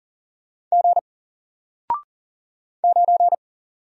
Morse code, heard as a keyed tone, about 40 wpm, 700 Hz. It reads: G 9
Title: Single Letter-Number 40wpm